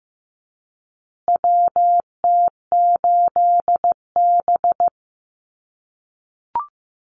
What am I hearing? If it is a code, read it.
Morse code, heard as a keyed tone, about 15 wpm, 700 Hz. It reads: WT8B
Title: USA Call Signs Repeated 3x 15wpm